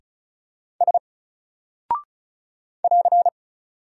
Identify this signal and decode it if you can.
Morse code, heard as a keyed tone, about 36 wpm, 700 Hz. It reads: S <AR>
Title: Single Character (Letter:Number:Punctuation:Prosign) 35wpm